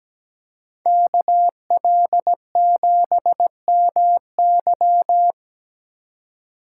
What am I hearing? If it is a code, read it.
Morse code, heard as a keyed tone, about 17 wpm, 700 Hz. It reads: KL7MY
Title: USA Call Signs - 5 Characters - 4x Character Spacing - 17wpm